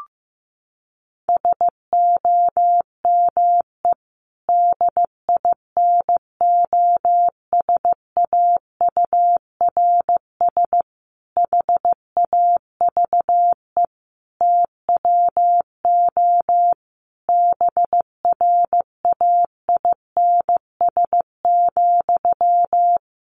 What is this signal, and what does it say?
Morse code, heard as a keyed tone, about 15 wpm, 700 Hz. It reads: SOME DINOSAURS HAVE TWO BRAINS,
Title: A Sound Of Thunder 15wpm